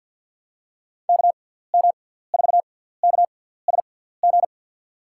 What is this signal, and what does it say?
Morse code, heard as a keyed tone, about 53 wpm, spaced as for 17 wpm, 700 Hz. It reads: KM4KSG
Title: USA Call Signs - 6 Characters - 6x Character Spacing - 50wpm